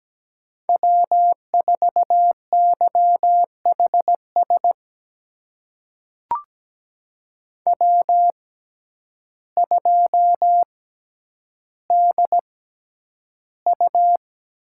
Morse code, heard as a keyed tone, about 17 wpm, spaced as for 9 wpm, 700 Hz. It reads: W4YHS W2DU
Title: USA Call Signs - 5 Characters - 6x Character Spacing - 17wpm